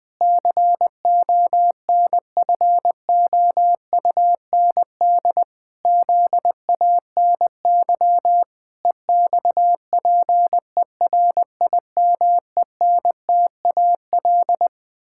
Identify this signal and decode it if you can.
Morse code, heard as a keyed tone, about 20 wpm, 700 Hz. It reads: CONFOUND ZANY EXPERIMENTAL
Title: Pangram Challenge v2 - 20wpm